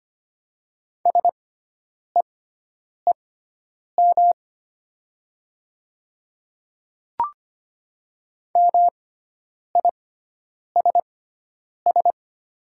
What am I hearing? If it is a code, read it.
Morse code, heard as a keyed tone, about 25 wpm, spaced as for 9 wpm, 700 Hz. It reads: SEEM MISS
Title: 6x Character Spacing - Top 500 Words - 25wpm